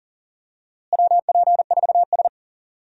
Morse code, heard as a keyed tone, about 38 wpm, 700 Hz. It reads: WP4S
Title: USA Call Signs - 4 Characters - Repeated 3x - 40wpm